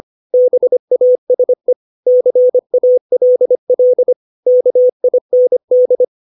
Morse code, heard as a keyed tone, about 25 wpm, 500 Hz. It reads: BASE CALL KIND